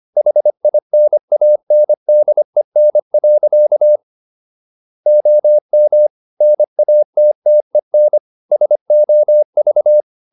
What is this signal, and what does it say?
Morse code, heard as a keyed tone, about 25 wpm, 600 Hz. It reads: HINANDEN. OM NATTEN SOV
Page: Ferie 25 wpm